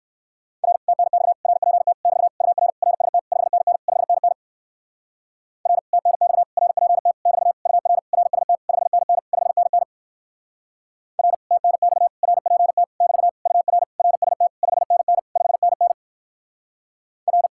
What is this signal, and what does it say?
Morse code, heard as a keyed tone, about 66 wpm, 700 Hz. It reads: R TNX RPT <BT> UR RST 5NN 5NN R TNX RPT <BT> UR RST 5NN 5NN R TNX RPT <BT> UR RST 5NN 5NN R